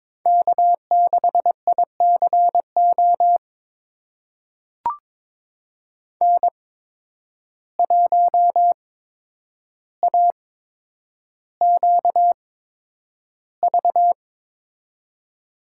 Morse code, heard as a keyed tone, about 22 wpm, spaced as for 9 wpm, 700 Hz. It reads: K6ICO N1AQV
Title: USA Call Signs - 5 Characters - 8x Character Spacing - 22wpm